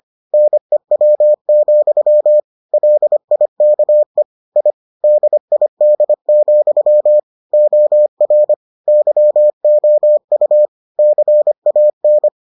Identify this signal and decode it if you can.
Morse code, heard as a keyed tone, about 25 wpm, 600 Hz. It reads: NEW, LIKE I DID, OR YOU CAN